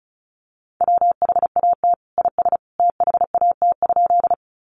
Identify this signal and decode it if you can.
Morse code, heard as a keyed tone, about 34 wpm, 700 Hz. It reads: WHAT IS THAT?
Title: Sentences from Top 100 Words 35wpm